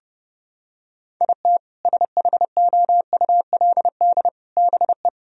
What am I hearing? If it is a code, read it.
Morse code, heard as a keyed tone, about 30 wpm, 700 Hz. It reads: IT SHOULD BE
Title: Sentences from Top 1000 Words - Review Part 2 - 30wpm